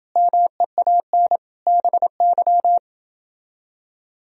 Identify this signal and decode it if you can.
Morse code, heard as a keyed tone, about 29 wpm, 700 Hz. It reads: MEAN BY